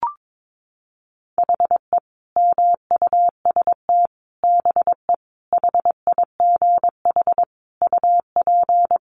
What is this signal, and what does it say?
Morse code, heard as a keyed tone, about 22 wpm, 700 Hz. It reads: HE MUST BE HIGH UP
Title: Sentences from Top 1200 Words 22wpm